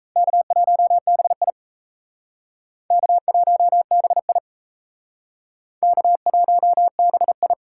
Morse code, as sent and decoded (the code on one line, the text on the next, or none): -.- .---- -... .. / -.- .---- -... .. / -.- .---- -... ..
K1BI K1BI K1BI